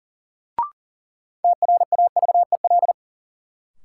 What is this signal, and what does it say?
Morse code, heard as a keyed tone, about 40 wpm, 700 Hz. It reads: TRAVEL
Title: Sentences from Top 400 Words - New Word Drill - 40wpm